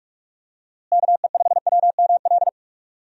Morse code, heard as a keyed tone, about 46 wpm, 700 Hz. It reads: KE5WML
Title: USA Call Signs - 6 Characters - Repeated 2x - 45wpm